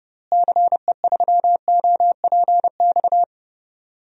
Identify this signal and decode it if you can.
Morse code, heard as a keyed tone, about 30 wpm, 700 Hz. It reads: CE3OPX
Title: DXCC-100 Call Signs - 6 Characters - Repeated 2x - 30wpm